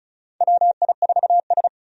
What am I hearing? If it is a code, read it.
Morse code, heard as a keyed tone, about 36 wpm, 700 Hz. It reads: WI4S
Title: USA Call Signs - 4 Characters - 35wpm